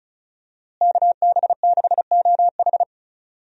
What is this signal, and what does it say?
Morse code, heard as a keyed tone, about 34 wpm, 700 Hz. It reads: KB6OH
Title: USA Call Signs - 5 Characters - Repeated 2x - 35wpm